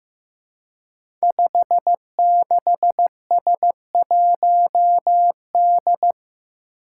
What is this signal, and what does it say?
Morse code, heard as a keyed tone, about 15 wpm, 700 Hz. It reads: 56S1D
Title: Five Challenging Letter-Number Combinations 15wpm